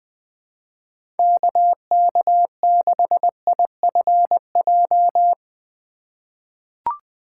Morse code, heard as a keyed tone, about 21 wpm, 700 Hz. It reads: KK6IFJ
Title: USA Call Signs - 6 Characters - 3x Character Spacing - 20wpm